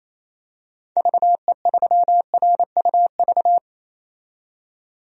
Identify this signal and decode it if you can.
Morse code, heard as a keyed tone, about 28 wpm, 700 Hz. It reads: VE3RUV
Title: DXCC-100 Call Signs - 6 Characters - 28wpm